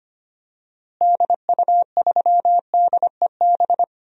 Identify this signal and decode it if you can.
Morse code, heard as a keyed tone, about 25 wpm, 700 Hz. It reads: DU3DEB
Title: DXCC-100 Call Signs - 6 Characters - Repeated 3x - 25wpm